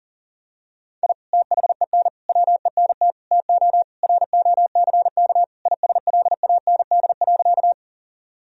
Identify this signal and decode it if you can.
Morse code, heard as a keyed tone, about 40 wpm, 700 Hz. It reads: I THEN WENT TO ROCK ISLAND.
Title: Sentences from Top 1000 Words - Review Part 3 - 40wpm